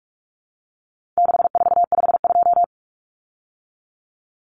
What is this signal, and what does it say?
Morse code, heard as a keyed tone, about 44 wpm, 700 Hz. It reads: 6452
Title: Common Cut Numbers - 4-Digit 45wpm